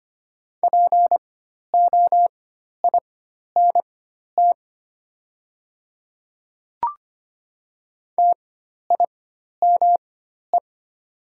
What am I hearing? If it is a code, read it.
Morse code, heard as a keyed tone, about 25 wpm, spaced as for 12 wpm, 700 Hz. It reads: POINT TIME
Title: 4x Character Spacing - Top 200 Words - 25wpm